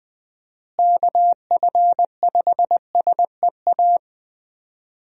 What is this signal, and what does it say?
Morse code, heard as a keyed tone, about 20 wpm, 700 Hz. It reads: KF5SEA
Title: USA Call Signs - 6 Characters - Repeated 3x - 20wpm